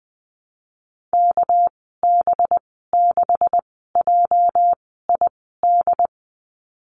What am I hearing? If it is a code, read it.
Morse code, heard as a keyed tone, about 20 wpm, spaced as for 14 wpm, 700 Hz. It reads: KB6JID
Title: 219 Lesson 35g - Number 0 - Call Signs 20wpm